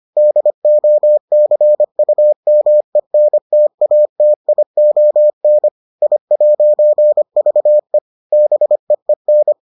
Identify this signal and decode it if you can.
Morse code, heard as a keyed tone, about 25 wpm, 600 Hz. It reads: DOCUMENTATION I'VE BEEN